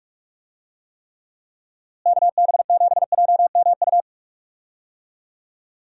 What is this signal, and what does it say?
Morse code, heard as a keyed tone, about 46 wpm, 700 Hz. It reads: KB7JMU
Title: USA Call Signs - 6 Characters - Repeated 3x - 45wpm